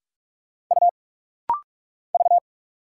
Morse code, heard as a keyed tone, about 46 wpm, 700 Hz. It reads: U V
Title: Single Letter-Number Alt - Rapid-Fire 45wpm